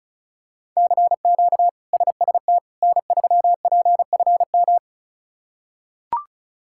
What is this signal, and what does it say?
Morse code, heard as a keyed tone, about 36 wpm, 700 Hz. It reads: CQ SST N3PFM
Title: K1USN SST QSO Practice - 35wpm